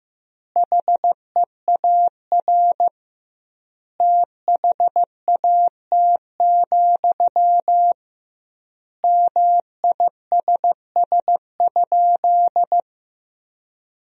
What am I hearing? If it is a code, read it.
Morse code, heard as a keyed tone, about 15 wpm, 700 Hz. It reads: HEAR THAT, MISS?
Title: 2x Word-Spacing - Sentences from Top 500 Words - 15wpm